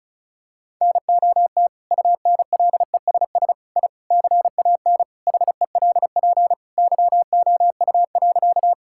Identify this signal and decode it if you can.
Morse code, heard as a keyed tone, about 34 wpm, 700 Hz. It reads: NOT UNLESS I CAN HELP YOU.
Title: Sentences from Top 1000 Words - Review Part 6 - 35wpm